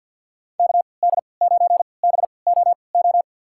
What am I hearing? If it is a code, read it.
Morse code, heard as a keyed tone, about 48 wpm, spaced as for 28 wpm, 700 Hz. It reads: KD9BOO
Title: USA Call Signs - 6 Characters - 3x Character Spacing - 50wpm